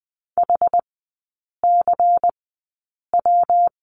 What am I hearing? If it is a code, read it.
Morse code, heard as a keyed tone, about 20 wpm, spaced as for 8 wpm, 700 Hz. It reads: HCW
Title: 138 Lesson 23e - Letter f - Three Characters 20wpm